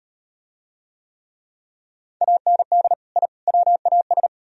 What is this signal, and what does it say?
Morse code, heard as a keyed tone, about 34 wpm, 700 Hz. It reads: AND I WAS